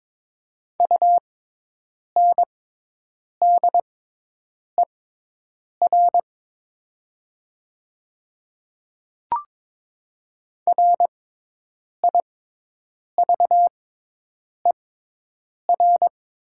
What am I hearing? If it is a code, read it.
Morse code, heard as a keyed tone, about 21 wpm, spaced as for 8 wpm, 700 Hz. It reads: UNDER RIVER
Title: 6x Character Spacing - Top 500 Words - 22wpm